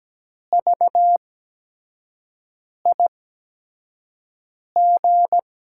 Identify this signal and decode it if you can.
Morse code, heard as a keyed tone, about 17 wpm, spaced as for 5 wpm, 700 Hz. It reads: VIG
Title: USA Call Signs - 8x Character Spacing - 17wpm